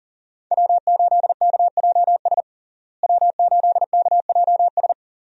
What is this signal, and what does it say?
Morse code, heard as a keyed tone, about 40 wpm, 700 Hz. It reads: W8KJS W8KJS
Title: USA Call Signs - 5 Characters - Repeated 3x - 40wpm